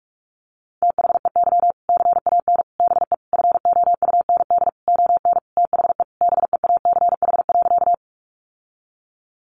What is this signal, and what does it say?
Morse code, heard as a keyed tone, about 44 wpm, 700 Hz. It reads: THEY CAN BE FOUND ON THE BEACH.
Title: Sentences from Top 1500 Words 45wpm